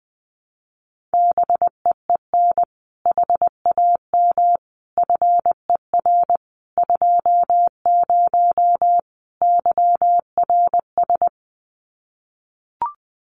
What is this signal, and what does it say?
Morse code, heard as a keyed tone, about 20 wpm, 700 Hz. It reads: BEEN HAM FER 20 YRS
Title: Common QSO Phrases 20wpm